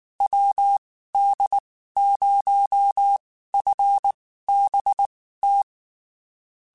Morse code, heard as a keyed tone, about 19 wpm, spaced as for 14 wpm, 800 Hz. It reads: WD0FBT
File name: QSOs / QSOs-mp3s / WD0FBT 1@18